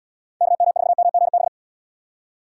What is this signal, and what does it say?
Morse code, heard as a keyed tone, about 72 wpm, 700 Hz. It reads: KM4MGB